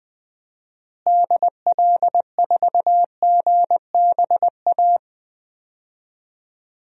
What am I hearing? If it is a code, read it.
Morse code, heard as a keyed tone, about 20 wpm, 700 Hz. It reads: DL4GBA
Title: DXCC-100 Call Signs - 6 Characters - Repeated 3x - 20wpm